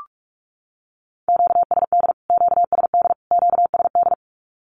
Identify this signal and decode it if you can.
Morse code, heard as a keyed tone, about 44 wpm, 700 Hz. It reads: QSD QSD QSD
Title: Q-Codes Repeated 3x 45wpm